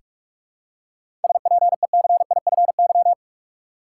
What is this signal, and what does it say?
Morse code, heard as a keyed tone, about 45 wpm, 700 Hz. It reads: SPECIFY
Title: Seven Letter Words - 45wpm